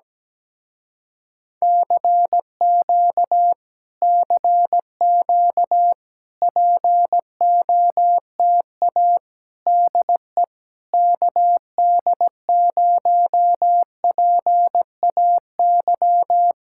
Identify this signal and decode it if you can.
Morse code, heard as a keyed tone, about 17 wpm, 700 Hz. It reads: CQ CQ POTA DE KD0PAY
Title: Speed-Racing Common QSO Phrases Encore - 17wpm